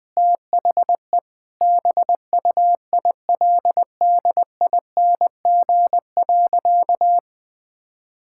Morse code, heard as a keyed tone, about 20 wpm, 700 Hz. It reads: THE BUILDING.
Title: Sentences from Top 1000 Words - Review Part 5 - 20wpm